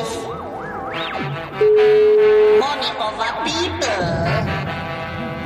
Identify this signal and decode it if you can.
Ringback tone (Europe, 425 Hz), with speech.